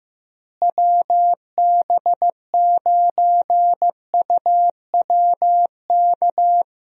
Morse code, heard as a keyed tone, about 15 wpm, 700 Hz. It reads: WB9UWK
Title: USA Call Signs - 6 Characters - Repeated 2x - 15wpm